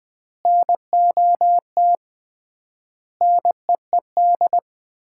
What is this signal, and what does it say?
Morse code, heard as a keyed tone, about 20 wpm, 700 Hz. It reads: NOT NEED
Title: 3x Word-Spacing - Sets of 4 Words - 20wpm